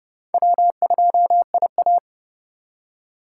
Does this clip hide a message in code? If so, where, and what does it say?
Yes: Morse code from 0.3 to 2.0 s, heard as a keyed tone, about 40 wpm, 700 Hz. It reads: W2IA